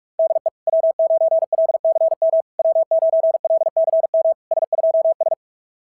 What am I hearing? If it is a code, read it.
Morse code, heard as a keyed tone, about 44 wpm, 660 Hz. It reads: DE W9LCM W9LCM S2S
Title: SOTA QSO Practice 45wpm